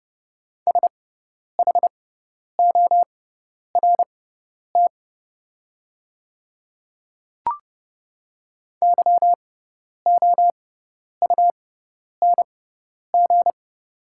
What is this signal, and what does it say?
Morse code, heard as a keyed tone, about 30 wpm, spaced as for 10 wpm, 690 Hz. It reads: SHORT YOUNG
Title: 6x Character Spacing - Top 500 Words - 30wpm